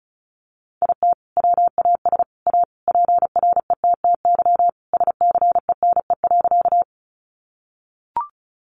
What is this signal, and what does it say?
Morse code, heard as a keyed tone, about 34 wpm, 700 Hz. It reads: IT WAS A PRETTY SCENE.
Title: Sentences from Top 1100 Words 35wpm